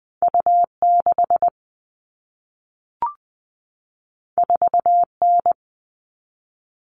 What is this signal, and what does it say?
Morse code, heard as a keyed tone, about 20 wpm, 700 Hz. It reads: U6 4N
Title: All Cut Numbers - 2-Digit 20wpm